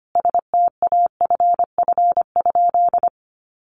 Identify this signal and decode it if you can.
Morse code, heard as a keyed tone, about 25 wpm, 700 Hz. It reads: STAFF?
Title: Sentences from Top 1200 Words 25wpm